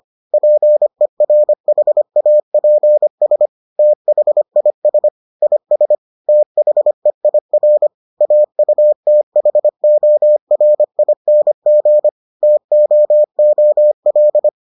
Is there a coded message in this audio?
Yes: Morse code from 0.3 to 14.5 s, heard as a keyed tone, about 25 wpm, 600 Hz. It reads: PERHAPS THIS IS THEIR AUTHORING TOOL